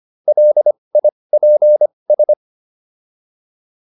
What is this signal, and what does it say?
Morse code, heard as a keyed tone, about 25 wpm, spaced as for 20 wpm, 600 Hz. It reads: LIPS